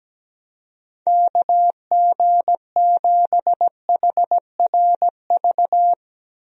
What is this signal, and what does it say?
Morse code, heard as a keyed tone, about 17 wpm, 700 Hz. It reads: KG7HRV
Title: USA Call Signs - 6 Characters - 17wpm